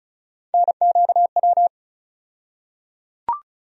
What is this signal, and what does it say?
Morse code, heard as a keyed tone, about 34 wpm, 700 Hz. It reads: NQW
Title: USA Call Sign Postfixes - 3 Letters 35wpm